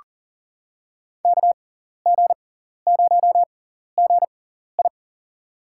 Morse code, heard as a keyed tone, about 40 wpm, spaced as for 14 wpm, 700 Hz. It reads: KG0GI
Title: USA Call Signs - 6x Character Spacing - 40wpm